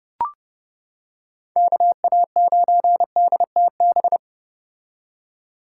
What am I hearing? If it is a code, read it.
Morse code, heard as a keyed tone, about 29 wpm, 700 Hz. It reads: KA9DTB